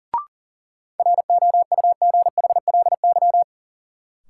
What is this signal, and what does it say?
Morse code, heard as a keyed tone, about 40 wpm, 700 Hz. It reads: ROUGHLY
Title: Sentences from Top 1500 Words - New Word Drill Alt - 40wpm